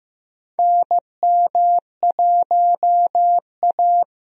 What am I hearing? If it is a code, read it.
Morse code, heard as a keyed tone, about 15 wpm, 700 Hz. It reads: NM1A
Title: USA Call Signs - 4 Characters - Repeated 2x - 15wpm